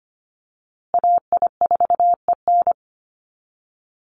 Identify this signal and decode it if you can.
Morse code, heard as a keyed tone, about 25 wpm, 700 Hz. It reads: AI4EN
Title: USA Call Signs Repeated 2x 25wpm